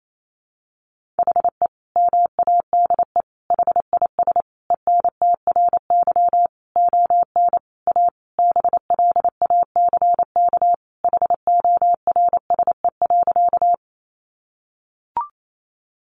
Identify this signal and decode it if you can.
Morse code, heard as a keyed tone, about 28 wpm, 700 Hz. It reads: HE MADE HIS ENTRY ON A BLACK HORSE.
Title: Sentences from Top 1500 Words 28wpm